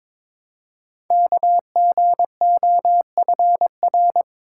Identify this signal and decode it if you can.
Morse code, heard as a keyed tone, about 22 wpm, 700 Hz. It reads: KGOFR
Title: Code Groups - 5 Letters - 22wpm